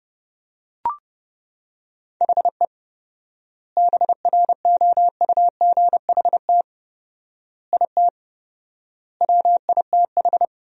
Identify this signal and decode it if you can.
Morse code, heard as a keyed tone, about 30 wpm, 700 Hz. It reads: HE BROUGHT IT WITH
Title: 4x Word-Spacing - Sentences from Top 500 Words - 30wpm